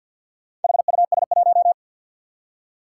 Morse code, heard as a keyed tone, about 54 wpm, 700 Hz. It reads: HUS1
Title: Four Challenging Letter-Number Combinations 50wpm